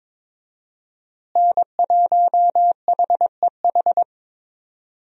Morse code, heard as a keyed tone, about 22 wpm, 700 Hz. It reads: N1HEH
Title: USA Call Signs - 5 Characters - Repeated 2x - 22wpm